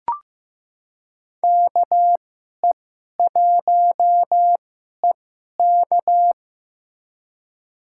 Morse code, heard as a keyed tone, about 15 wpm, spaced as for 11 wpm, 700 Hz. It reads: KE1EK
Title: USA Call Signs - 5 Characters - 2x Character Spacing - 15wpm